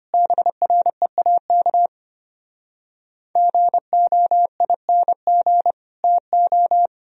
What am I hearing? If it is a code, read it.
Morse code, heard as a keyed tone, about 33 wpm, 700 Hz. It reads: BREAK GOING TO